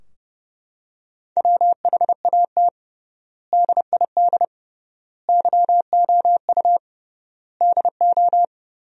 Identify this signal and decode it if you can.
Morse code, heard as a keyed tone, about 29 wpm, 700 Hz. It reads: WHAT DID YOU DO
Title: 3x Word-Spacing - Sentences from Top 100 Words - 30wpm